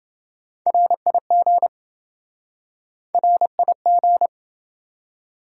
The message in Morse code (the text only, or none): RIG RIG